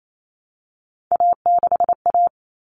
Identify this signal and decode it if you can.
Morse code, heard as a keyed tone, about 28 wpm, 700 Hz. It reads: A6A
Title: Common Cut Numbers - 3-Digit 28wpm